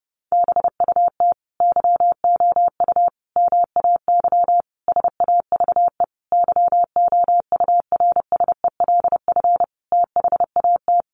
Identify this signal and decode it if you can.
Morse code, heard as a keyed tone, about 30 wpm, 700 Hz. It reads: BUT YOU MAY SAVE YOURSELF THAT
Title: Sentences from Top 1100 Words 30wpm